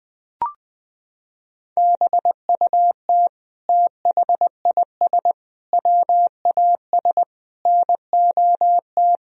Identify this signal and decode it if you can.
Morse code, heard as a keyed tone, about 20 wpm, 700 Hz. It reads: BUT THIS WAS NOT
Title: Sentences from Top 1000 Words - Review Part 3 - 20wpm